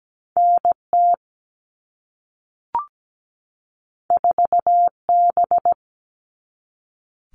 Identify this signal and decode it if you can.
Morse code, heard as a keyed tone, about 17 wpm, 700 Hz. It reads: NT 4B
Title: All Cut Numbers - 2-Digit 17wpm